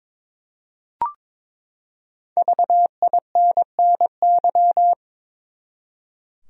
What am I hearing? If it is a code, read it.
Morse code, heard as a keyed tone, about 22 wpm, 700 Hz. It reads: VINNY
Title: USA Common Names - Max 5 Letters - 22wpm